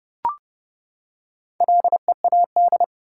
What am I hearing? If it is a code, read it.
Morse code, heard as a keyed tone, about 31 wpm, 700 Hz. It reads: LEAD
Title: Four Letter Words - 30wpm